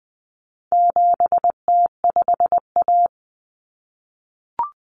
Morse code, heard as a keyed tone, about 20 wpm, 700 Hz. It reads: 7T5A
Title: Common Cut Numbers - 4-Digit 20wpm